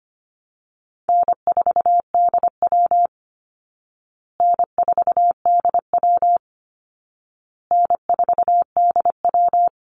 Morse code, heard as a keyed tone, about 25 wpm, 700 Hz. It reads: N4DW N4DW N4DW
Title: Call Signs Repeated 2x - 25wpm